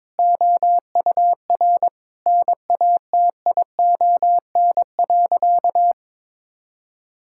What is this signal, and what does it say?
Morse code, heard as a keyed tone, about 21 wpm, 700 Hz. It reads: OUR NATION.
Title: Sentences from Top 1000 Words - Review Part 6 - 22wpm